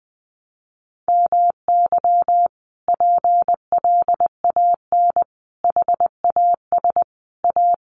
Morse code, heard as a keyed tone, about 20 wpm, 700 Hz. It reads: MY PLAN HAS A
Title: Sentences from Top 1300 Words 20wpm